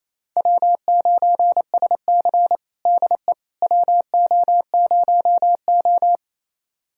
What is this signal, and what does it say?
Morse code, heard as a keyed tone, about 28 wpm, 700 Hz. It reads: W9SC DE WO0O
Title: USA Callsign DE Callsign - 4 Characters - 28wpm